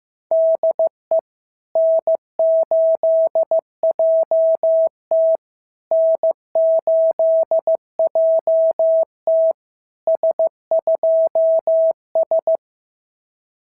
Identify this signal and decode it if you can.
Morse code, heard as a keyed tone, about 14 wpm, 650 Hz. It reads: DE N8JT N8JT S2S
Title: SOTA QSO Practice 15wpm